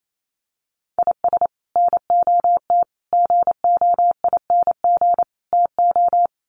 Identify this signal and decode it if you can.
Morse code, heard as a keyed tone, about 28 wpm, 700 Hz. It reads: IS NOT GOING TO
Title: Sets of 4 Words 28wpm